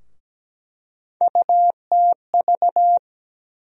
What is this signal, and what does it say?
Morse code, heard as a keyed tone, about 17 wpm, 700 Hz. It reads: UTV
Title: USA Call Sign Postfixes - 3 Letters 17wpm